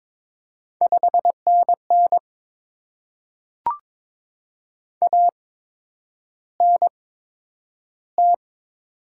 Morse code, heard as a keyed tone, about 22 wpm, spaced as for 9 wpm, 700 Hz. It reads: 5NN ANT
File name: Top 20 QSO Elements - 8x Character Spacing - 22wpm